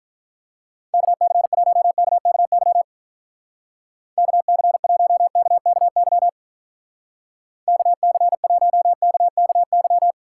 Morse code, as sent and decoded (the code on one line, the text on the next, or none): -.- -.-. .---- -.- -.- -.-- / -.- -.-. .---- -.- -.- -.-- / -.- -.-. .---- -.- -.- -.--
KC1KKY KC1KKY KC1KKY